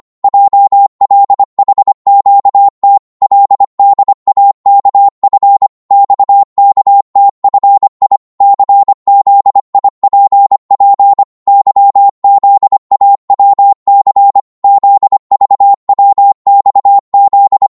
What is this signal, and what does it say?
Morse code, heard as a keyed tone, about 25 wpm, 800 Hz. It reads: JLHQT LDAKF XKTFI CZIPP YZAWC ZVWXZ
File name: Morse / ltr-25wpm